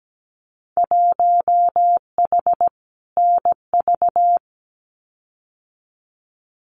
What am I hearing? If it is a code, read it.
Morse code, heard as a keyed tone, about 17 wpm, 700 Hz. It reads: 1H NV
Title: Winter Field Day Exchanges 17wpm